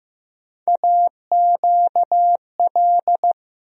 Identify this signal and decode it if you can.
Morse code, heard as a keyed tone, about 15 wpm, 700 Hz. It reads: AQL
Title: USA Call Sign Postfixes - 3 Letters 15wpm